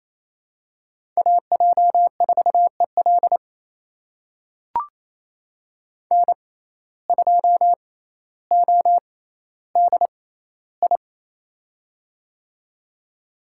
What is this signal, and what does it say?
Morse code, heard as a keyed tone, about 28 wpm, spaced as for 14 wpm, 700 Hz. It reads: AJ4EL N2ODI
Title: USA Call Signs - 6x Character Spacing - 28wpm